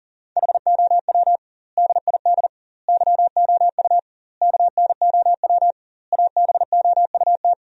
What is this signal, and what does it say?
Morse code, heard as a keyed tone, about 40 wpm, 700 Hz. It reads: HOW DID YOU KNOW ABOUT
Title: 2x Word-Spacing - Sentences from Top 100 Words - 40wpm